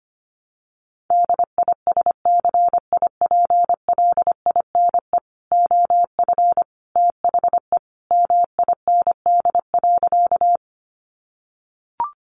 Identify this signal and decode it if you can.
Morse code, heard as a keyed tone, about 25 wpm, 700 Hz. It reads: DISCIPLINE OF THE MIND.
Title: Sentences from Top 900 Words 25wpm